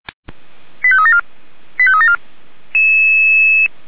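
Five-tone selective call (ZVEI), heard as quick runs of five tones, tones of about 70 ms. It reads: 85374 85374
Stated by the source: ZVEI five-tone selective call received on a software-defined radio.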